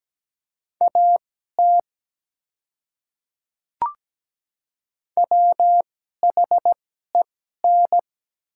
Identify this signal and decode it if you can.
Morse code, heard as a keyed tone, about 17 wpm, spaced as for 12 wpm, 700 Hz. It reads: AT WHEN
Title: 2x Character Spacing - Top 100 Words - 17wpm